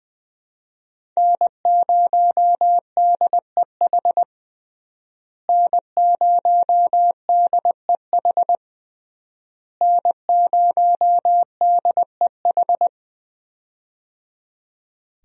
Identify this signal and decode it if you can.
Morse code, heard as a keyed tone, about 19 wpm, 700 Hz. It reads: N0DEH N0DEH N0DEH
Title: USA Call Signs - 5 Characters - Repeated 3x - 20wpm